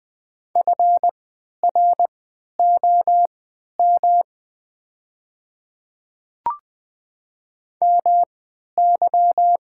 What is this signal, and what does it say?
Morse code, heard as a keyed tone, about 20 wpm, spaced as for 11 wpm, 700 Hz. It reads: FROM MY
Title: 3x Character Spacing - Top 100 Words - 20wpm